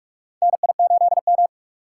Morse code, heard as a keyed tone, about 45 wpm, 700 Hz. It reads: NI8M
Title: USA Call Signs - 4 Characters - 4x Character Spacing - 45wpm